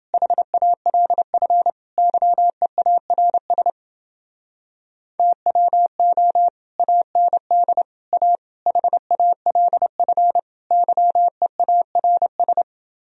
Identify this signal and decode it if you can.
Morse code, heard as a keyed tone, about 30 wpm, 700 Hz. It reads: HALF YEARS TWO AND A HALF YEARS